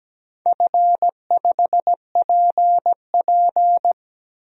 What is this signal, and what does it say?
Morse code, heard as a keyed tone, about 17 wpm, 700 Hz. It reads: F5PP
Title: DXCC-100 Call Signs - 4 Characters - 17wpm